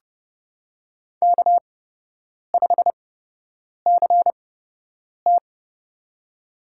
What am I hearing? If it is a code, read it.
Morse code, heard as a keyed tone, about 30 wpm, spaced as for 8 wpm, 700 Hz. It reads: K5CT
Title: USA Call Signs - 4 Characters - 8x Character Spacing - 30wpm